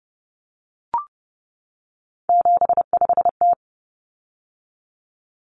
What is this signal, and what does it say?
Morse code, heard as a keyed tone, about 30 wpm, 700 Hz. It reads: 75T
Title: Common Cut Numbers - 3-Digit 30wpm